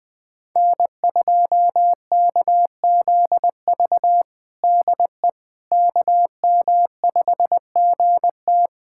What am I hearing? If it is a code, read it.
Morse code, heard as a keyed tone, about 20 wpm, 700 Hz. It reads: N2KZV DE KM5GT
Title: USA Callsign DE Callsign - 5 Characters - 20wpm